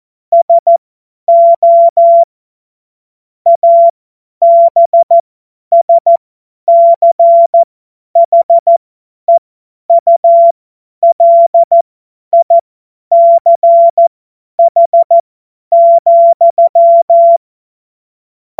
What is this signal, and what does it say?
Morse code, heard as a keyed tone, about 14 wpm, spaced as for 10 wpm, 680 Hz. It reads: SO ABSCHEULICH,
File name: Morsekurs / Witze / Witz 10. Mai 2022 50 F70